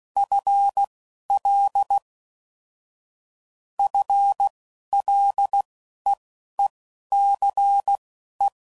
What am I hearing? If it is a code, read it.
Morse code, heard as a keyed tone, about 16 wpm, spaced as for 11 wpm, 800 Hz. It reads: FL FLEECE
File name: Prefixes / FLs & THs / FL Prefixes 3